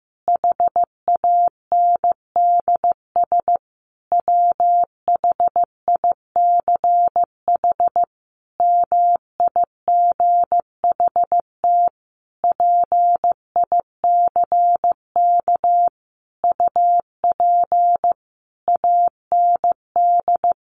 Morse code, heard as a keyed tone, about 14 wpm, 700 Hz. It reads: HANDS WHICH MIGHT PICK UP AND
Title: A Sound Of Thunder 15wpm